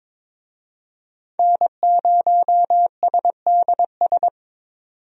Morse code, heard as a keyed tone, about 21 wpm, 700 Hz. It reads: N0SDS
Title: USA Call Signs - 5 Characters - 22wpm